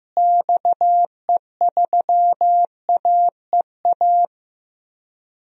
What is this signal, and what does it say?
Morse code, heard as a keyed tone, about 16 wpm, 700 Hz. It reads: XE3AEA
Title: DXCC-100 Call Signs - 6 Characters - Repeated 3x - 15wpm